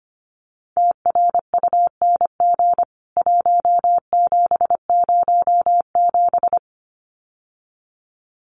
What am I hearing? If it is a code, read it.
Morse code, heard as a keyed tone, about 25 wpm, 700 Hz. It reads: TRUNG 1707
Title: CWT Exchange 25wpm